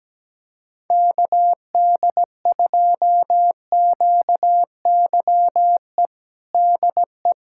Morse code, heard as a keyed tone, about 17 wpm, 700 Hz. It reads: KD2QYE DE
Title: USA Callsign DE Callsign 17wpm